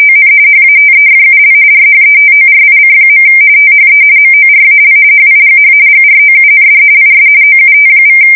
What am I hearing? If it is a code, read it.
RTTY, about 45.45 baud. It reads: ON HATEFUL RHETORIC. THE MORAL HIGH GROUND IS OUT=
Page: 170 Shift 2125/2295